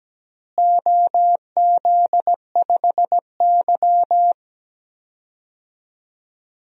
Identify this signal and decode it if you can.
Morse code, heard as a keyed tone, about 17 wpm, 700 Hz. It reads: OZ5Y
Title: DXCC-100 Call Signs - 4 Characters - Repeated 2x - 17wpm